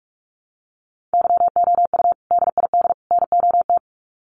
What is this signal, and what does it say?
Morse code, heard as a keyed tone, about 44 wpm, 700 Hz. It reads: YOU DID NOT
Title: Sets of 3 Words 45wpm